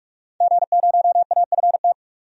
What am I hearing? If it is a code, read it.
Morse code, heard as a keyed tone, about 46 wpm, 700 Hz. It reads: G0AFT
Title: DXCC-100 Call Signs - 5 Characters - 45wpm